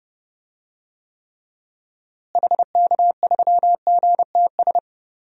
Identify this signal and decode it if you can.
Morse code, heard as a keyed tone, about 32 wpm, 700 Hz. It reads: HK3GTS
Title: DXCC-100 Call Signs - 6 Characters - Repeated 3x - 30wpm